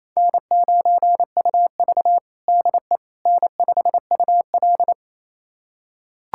Morse code, heard as a keyed tone, about 28 wpm, 700 Hz. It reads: N9UV DE N5UL
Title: USA Callsign DE Callsign - 4 Characters - 28wpm